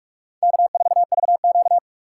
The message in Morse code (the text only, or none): K4VQ